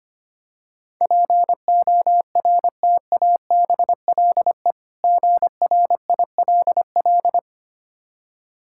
Morse code, heard as a keyed tone, about 26 wpm, 700 Hz. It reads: PORTABLE GRILL
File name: practice / 25wpm / Difficult Words - Speed-Racing with Hints - 25wpm